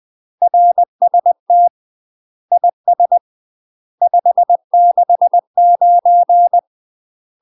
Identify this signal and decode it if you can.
Morse code, heard as a keyed tone, about 20 wpm, 700 Hz. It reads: RST IS 569
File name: UK MASTER QSOs TRAINING / 20WPM / UK MASTER QSOs 0018